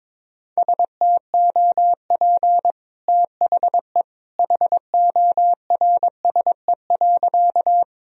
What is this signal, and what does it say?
Morse code, heard as a keyed tone, about 22 wpm, 700 Hz. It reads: STOP THE HORSE.
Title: Sentences from Top 1000 Words - Review Part 1 - 22wpm